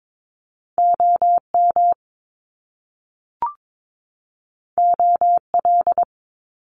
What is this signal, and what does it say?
Morse code, heard as a keyed tone, about 22 wpm, 700 Hz. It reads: OM OL
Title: Top Two Letter Combinations in Words 22wpm